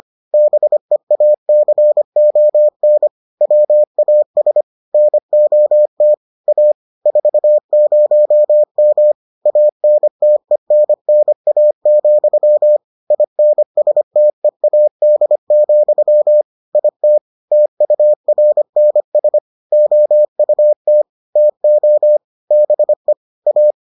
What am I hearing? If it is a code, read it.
Morse code, heard as a keyed tone, about 25 wpm, 600 Hz. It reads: BEACON WAS NOT A 40M ANTENNA, INSTEAD, IT TURNS OUT TO BE A